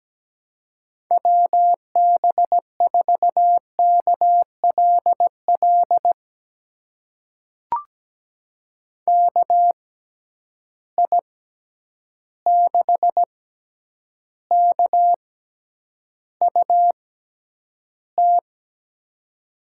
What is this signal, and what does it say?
Morse code, heard as a keyed tone, about 17 wpm, spaced as for 9 wpm, 700 Hz. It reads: WB4KLL KI6KUT
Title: USA Call Signs - 6 Characters - 6x Character Spacing - 17wpm